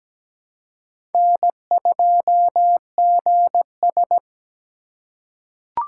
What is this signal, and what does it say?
Morse code, heard as a keyed tone, about 17 wpm, 700 Hz. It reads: N2GS
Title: USA Call Signs - 4 Characters - 2x Character Spacing - 17wpm